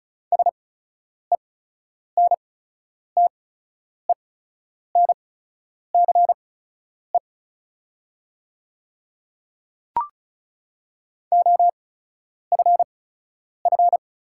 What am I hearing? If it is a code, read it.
Morse code, heard as a keyed tone, about 34 wpm, spaced as for 10 wpm, 700 Hz. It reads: SENTENCE OFF
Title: 8x Character Spacing - Top 200 Words - 35wpm